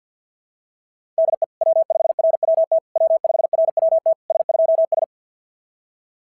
Morse code, heard as a keyed tone, about 46 wpm, 660 Hz. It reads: DE W5RWT W5RWT S2S
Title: SOTA QSO Practice 50wpm